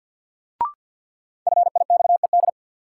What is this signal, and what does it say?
Morse code, heard as a keyed tone, about 52 wpm, 700 Hz. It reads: FIXED
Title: Sentences from Top 1100 Words - New Word Drill Alt - 50wpm